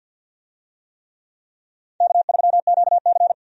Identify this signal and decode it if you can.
Morse code, heard as a keyed tone, about 51 wpm, 700 Hz. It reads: K3QC
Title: USA Call Signs - 4 Characters - 50wpm